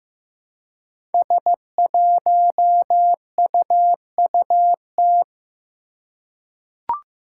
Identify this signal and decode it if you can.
Morse code, heard as a keyed tone, about 15 wpm, 700 Hz. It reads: S1UUT
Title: Five Challenging Letter-Number Combinations 15wpm